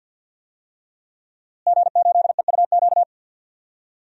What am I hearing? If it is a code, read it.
Morse code, heard as a keyed tone, about 51 wpm, 700 Hz. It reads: G8EUQ
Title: DXCC-100 Call Signs - 5 Characters - Repeated 3x - 50wpm